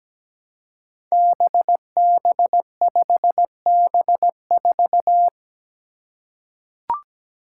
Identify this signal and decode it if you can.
Morse code, heard as a keyed tone, about 17 wpm, 700 Hz. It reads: BB5B4
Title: Five Challenging Letter-Number Combinations 17wpm